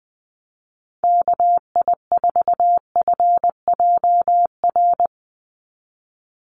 Morse code, heard as a keyed tone, about 20 wpm, 700 Hz. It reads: KI4FJR
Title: USA Call Signs 20wpm